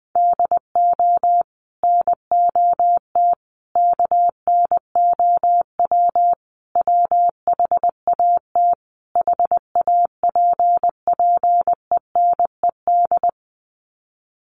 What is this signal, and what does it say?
Morse code, heard as a keyed tone, about 20 wpm, 700 Hz. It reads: DO NOT KNOW WHAT HAPPENED
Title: Sets of 5 Words Encore2 20wpm